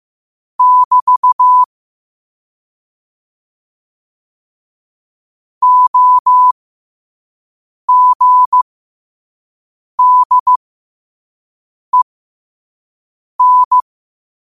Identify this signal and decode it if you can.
Morse code, heard as a keyed tone, about 15 wpm, spaced as for 5 wpm, 1000 Hz. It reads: <BT> OGDEN
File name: qotdfiles / 5wpm